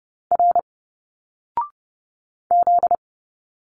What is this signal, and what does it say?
Morse code, heard as a keyed tone, about 30 wpm, 700 Hz. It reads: R Z
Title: Single Letters 30wpm